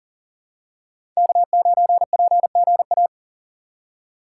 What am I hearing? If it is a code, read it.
Morse code, heard as a keyed tone, about 40 wpm, 700 Hz. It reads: K9PGA
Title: USA Call Signs - 3x Character Spacing - 40wpm